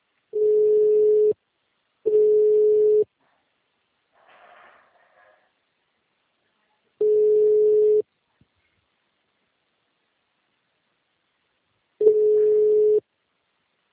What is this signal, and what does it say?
Ringback tone (Europe, 425 Hz).